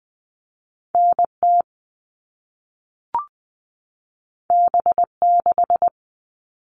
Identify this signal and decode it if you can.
Morse code, heard as a keyed tone, about 20 wpm, 700 Hz. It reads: N T B 6
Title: All Cut Numbers - 2-Digit 20wpm